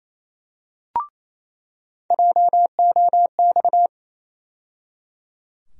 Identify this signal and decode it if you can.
Morse code, heard as a keyed tone, about 28 wpm, 700 Hz. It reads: JOX
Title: USA Call Sign Postfixes - 3 Letters 28wpm